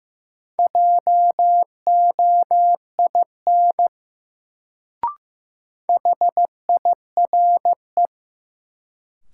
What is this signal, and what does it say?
Morse code, heard as a keyed tone, about 15 wpm, 700 Hz. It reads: JOIN HIRE
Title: Sentences from Top 1400 Words - New Word Drill Alt - 15wpm